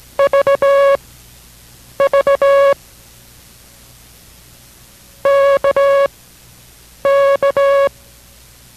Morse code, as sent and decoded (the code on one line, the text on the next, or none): ...- ...- / -.- -.-
VV KK